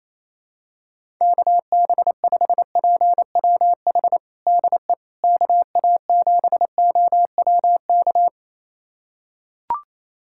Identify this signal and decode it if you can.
Morse code, heard as a keyed tone, about 28 wpm, 700 Hz. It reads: KB5PWH DE KA7OWK
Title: USA Callsign DE Callsign 28wpm